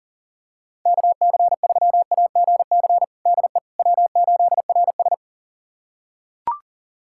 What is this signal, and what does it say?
Morse code, heard as a keyed tone, about 40 wpm, 700 Hz. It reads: KC3AGC DE W8RS
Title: USA Callsign DE Callsign 40wpm